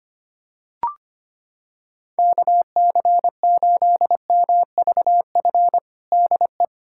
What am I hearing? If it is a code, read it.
Morse code, heard as a keyed tone, about 25 wpm, 700 Hz. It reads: KC8MVF DE
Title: USA Callsign DE Callsign 25wpm